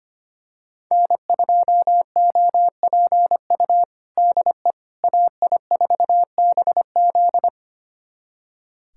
Morse code, heard as a keyed tone, about 25 wpm, 700 Hz. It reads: N2OPU DE AI4BZ
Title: USA Callsign DE Callsign 25wpm